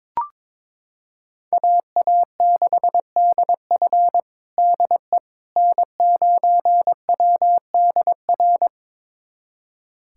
Morse code, heard as a keyed tone, about 22 wpm, 700 Hz. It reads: AA6DF DE N9WDR
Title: USA Callsign DE Callsign - 5 Characters - 22wpm